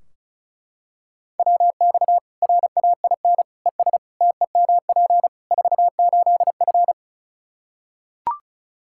Morse code, heard as a keyed tone, about 36 wpm, 700 Hz. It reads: WX RAIN ES TEMP 48F
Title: Practice QSO - Exchange 2 - Info - 35wpm